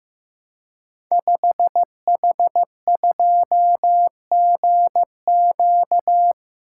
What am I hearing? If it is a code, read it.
Morse code, heard as a keyed tone, about 15 wpm, 700 Hz. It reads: 5H2GQ
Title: Code Groups - 5 Letter:Numbers - 15wpm